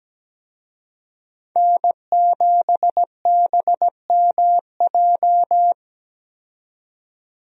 Morse code, heard as a keyed tone, about 17 wpm, 700 Hz. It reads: N7BMJ
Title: USA Call Signs - 5 Characters - 17wpm